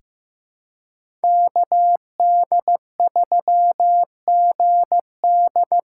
Morse code, heard as a keyed tone, about 15 wpm, 700 Hz. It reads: KD3GD
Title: USA Callsign DE Callsign - 5 Characters - 15wpm